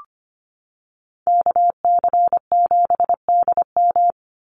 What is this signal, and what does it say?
Morse code, heard as a keyed tone, about 25 wpm, 700 Hz. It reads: KC7DM
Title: NA SOTA Activator Callsigns 25wpm